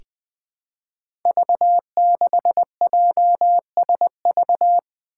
Morse code, heard as a keyed tone, about 20 wpm, 700 Hz. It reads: V6JSV
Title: Five Challenging Letter-Number Combinations 20wpm